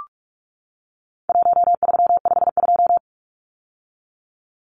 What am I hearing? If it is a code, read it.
Morse code, heard as a keyed tone, about 44 wpm, 700 Hz. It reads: 1352
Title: Numbers - Contest Style - 4-Digit 45wpm